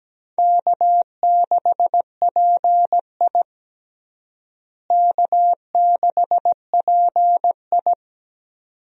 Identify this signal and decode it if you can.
Morse code, heard as a keyed tone, about 17 wpm, 700 Hz. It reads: K6PI K6PI
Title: USA Call Signs - 4 Characters - Repeated 2x - 17wpm